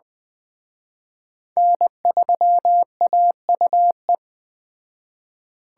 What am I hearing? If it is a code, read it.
Morse code, heard as a keyed tone, about 20 wpm, 700 Hz. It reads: N3AUE
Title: Speed Racing - USA Call Signs 17wpm